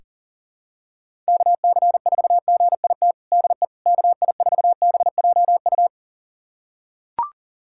Morse code, heard as a keyed tone, about 37 wpm, 700 Hz. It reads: KC4GIT DE KI4BJU
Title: USA Callsign DE Callsign 40wpm